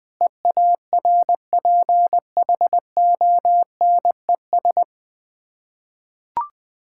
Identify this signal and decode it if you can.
Morse code, heard as a keyed tone, about 20 wpm, 690 Hz. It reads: EARPHONES
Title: Difficult Words with Hints - 20wpm